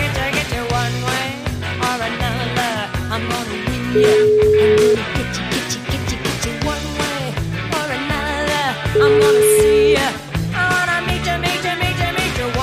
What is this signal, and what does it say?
Ringback tone (Europe, 425 Hz), with noise.